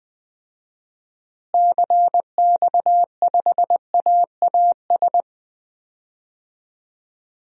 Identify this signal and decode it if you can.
Morse code, heard as a keyed tone, about 21 wpm, 700 Hz. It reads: CX5AAS
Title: DXCC-100 Call Signs - 6 Characters - Repeated 3x - 20wpm